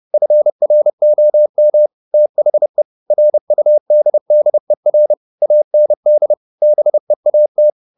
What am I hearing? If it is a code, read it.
Morse code, heard as a keyed tone, about 30 wpm, 600 Hz. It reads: FROM THE RUDDER AND BEAT